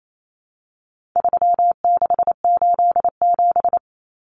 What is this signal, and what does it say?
Morse code, heard as a keyed tone, about 28 wpm, 700 Hz. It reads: 3687
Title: Numbers - 4-Digit 28wpm